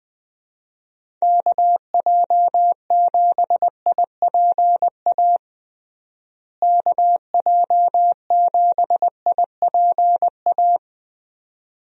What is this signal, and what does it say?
Morse code, heard as a keyed tone, about 21 wpm, 700 Hz. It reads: KJ7IPA KJ7IPA
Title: USA Call Signs - 6 Characters - Repeated 2x - 20wpm